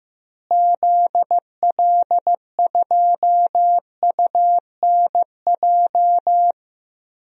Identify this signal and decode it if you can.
Morse code, heard as a keyed tone, about 16 wpm, 700 Hz. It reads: ZL2UNJ
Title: DXCC-100 Call Signs - 6 Characters - Repeated 3x - 15wpm